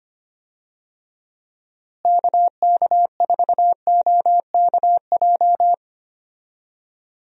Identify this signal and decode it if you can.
Morse code, heard as a keyed tone, about 25 wpm, 700 Hz. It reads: KK4OKJ
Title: USA Call Signs - 6 Characters - Repeated 2x - 25wpm